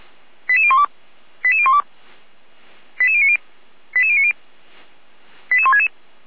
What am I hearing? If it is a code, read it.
Five-tone selective call (ZVEI), heard as quick runs of five tones, tones of about 70 ms. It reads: 88012 88012 88090 88090 88166
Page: /455/SDR/ZVEI/88352/2022.12.31/